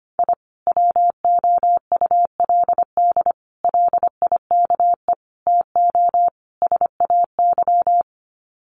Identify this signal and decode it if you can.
Morse code, heard as a keyed tone, about 25 wpm, 700 Hz. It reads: I WOULD LIKE TO SAY
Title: Sets of 5 Words Encore2 25wpm